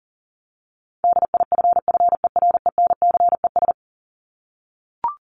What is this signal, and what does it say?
Morse code, heard as a keyed tone, about 37 wpm, 700 Hz. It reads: DIFFERENCES
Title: ES:S Suffix - Top 100 Words with Base Words 40wpm